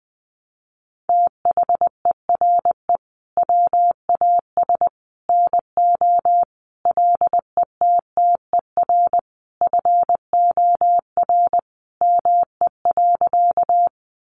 Morse code, heard as a keyed tone, about 20 wpm, 700 Hz. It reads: THERE WAS NO LETTER FOR ME.
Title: Sentences from Top 300 Words 20wpm